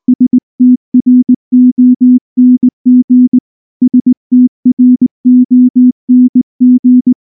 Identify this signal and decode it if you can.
Morse code, heard as a keyed tone, about 19 wpm, 250 Hz. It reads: STRONG STRONG
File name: morse-training-20WPM